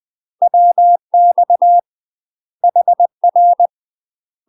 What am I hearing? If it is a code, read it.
Morse code, heard as a keyed tone, about 20 wpm, 700 Hz. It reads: WX HR
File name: UK MASTER QSOs TRAINING / 20WPM / UK MASTER QSOs 0060